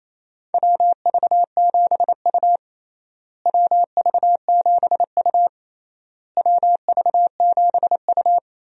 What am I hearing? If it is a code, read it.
Morse code, heard as a keyed tone, about 28 wpm, 700 Hz. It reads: WV7U WV7U WV7U
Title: USA Call Signs - 4 Characters - Repeated 3x - 28wpm